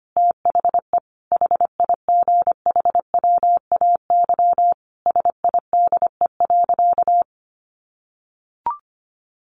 Morse code, heard as a keyed tone, about 25 wpm, 700 Hz. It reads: THE HIGHWAY SIDE.
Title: Sentences from Top 1400 Words 25wpm